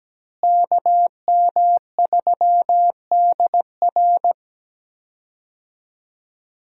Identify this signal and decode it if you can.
Morse code, heard as a keyed tone, about 17 wpm, 700 Hz. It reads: KM3DR
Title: USA Call Signs - 5 Characters - Repeated 3x - 17wpm